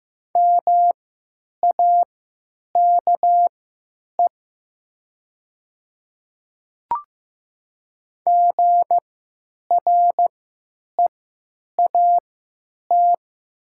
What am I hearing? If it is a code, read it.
Morse code, heard as a keyed tone, about 15 wpm, spaced as for 9 wpm, 700 Hz. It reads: MAKE GREAT
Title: 3x Character Spacing - Top 300 Words - 15wpm